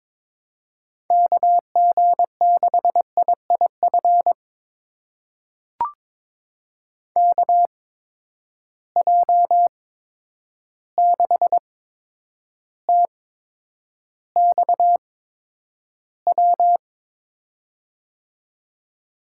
Morse code, heard as a keyed tone, about 21 wpm, spaced as for 9 wpm, 700 Hz. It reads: KG6IIF KJ6TXW
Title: USA Call Signs - 6 Characters - 8x Character Spacing - 22wpm